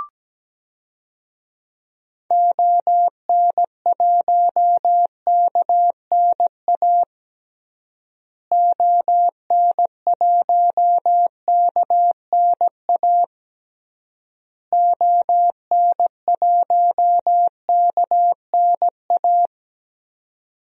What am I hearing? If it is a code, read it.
Morse code, heard as a keyed tone, about 17 wpm, 700 Hz. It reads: ON1KNA ON1KNA ON1KNA
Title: DXCC-100 Call Signs - 6 Characters - Repeated 3x - 17wpm